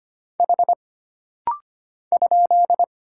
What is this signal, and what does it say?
Morse code, heard as a keyed tone, about 25 wpm, 700 Hz. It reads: H ?
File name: Single Character (Letter:Number:Punctuation:Prosign) - Rapid-Fire - 25wpm